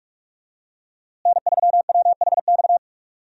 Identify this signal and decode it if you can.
Morse code, heard as a keyed tone, about 45 wpm, 700 Hz. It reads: N3WHX
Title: USA Call Signs - 5 Characters - Repeated 3x - 45wpm